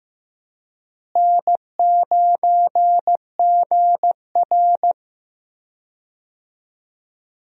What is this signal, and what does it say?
Morse code, heard as a keyed tone, about 15 wpm, 700 Hz. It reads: N9GR
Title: USA Call Signs - 4 Characters - Repeated 3x - 15wpm